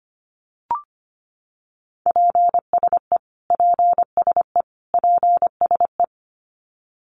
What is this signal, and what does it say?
Morse code, heard as a keyed tone, about 25 wpm, 700 Hz. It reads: PSE PSE PSE
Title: Abbreviations Repeated 3x - 25wpm